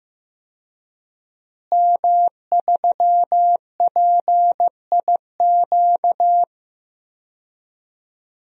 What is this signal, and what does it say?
Morse code, heard as a keyed tone, about 15 wpm, 700 Hz. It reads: M3PIQ
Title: DXCC-100 Call Signs - 5 Characters - Repeated 2x - 15wpm